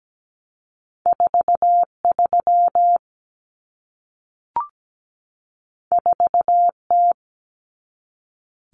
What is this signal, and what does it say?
Morse code, heard as a keyed tone, about 17 wpm, 700 Hz. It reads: 43 4T
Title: Common Cut Numbers - 2-Digit 17wpm